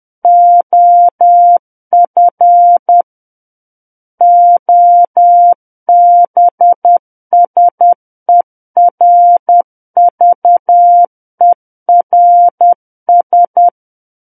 Morse code, heard as a keyed tone, about 10 wpm, 700 Hz. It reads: OF OBSERVERS
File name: Book One / 01-Chapter 1 - 10 WPM